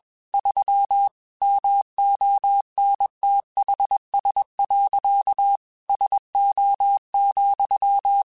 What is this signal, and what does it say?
Morse code, heard as a keyed tone, about 21 wpm, 800 Hz. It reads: 3 MONTHS. SO,
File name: QSOs / QSOs-mp3s / WD0BGP 1@20